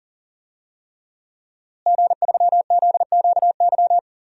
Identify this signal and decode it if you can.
Morse code, heard as a keyed tone, about 39 wpm, 700 Hz. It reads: G3ZQY
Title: DXCC-100 Call Signs - 5 Characters - Repeated 3x - 40wpm